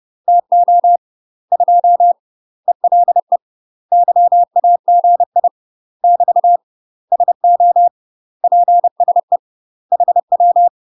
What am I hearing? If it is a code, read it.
Morse code, heard as a keyed tone, about 30 wpm, 700 Hz. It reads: TO 2 ELE YAGI <BT> SO PSE HW
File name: UK MASTER QSOs TRAINING / 30WPM / UK MASTER QSOs 0026